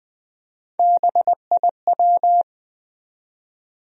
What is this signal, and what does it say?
Morse code, heard as a keyed tone, about 20 wpm, 700 Hz. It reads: BIW
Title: USA Call Sign Postfixes - 3 Letters 20wpm